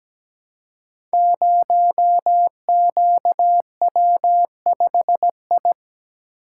Morse code, heard as a keyed tone, about 17 wpm, 700 Hz. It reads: 0QW5I
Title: Code Groups - 5 Letter:Numbers - 17wpm